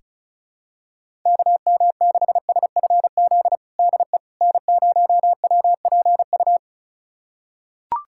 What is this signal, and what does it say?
Morse code, heard as a keyed tone, about 36 wpm, 690 Hz. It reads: KM6SFZ DE N0WPU
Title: USA Callsign DE Callsign 35wpm